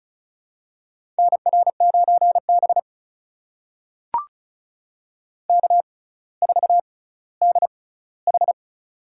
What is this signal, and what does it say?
Morse code, heard as a keyed tone, about 36 wpm, spaced as for 18 wpm, 700 Hz. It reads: NR9B K4DH
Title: USA Call Signs - 4 Characters - 6x Character Spacing - 35wpm